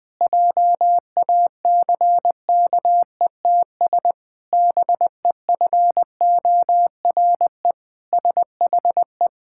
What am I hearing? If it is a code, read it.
Morse code, heard as a keyed tone, about 20 wpm, 700 Hz. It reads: JACKETS BEFORE SHE
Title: Pangram Challenge v2 - 20wpm